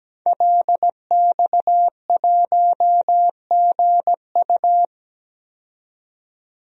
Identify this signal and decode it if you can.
Morse code, heard as a keyed tone, about 17 wpm, 700 Hz. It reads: LX1GU
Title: DXCC-100 Call Signs - 5 Characters - 17wpm